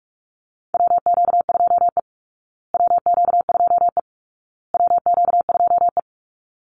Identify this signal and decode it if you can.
Morse code, heard as a keyed tone, about 44 wpm, 700 Hz. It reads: WQ2E WQ2E WQ2E
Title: Call Signs Repeated 2x - 45wpm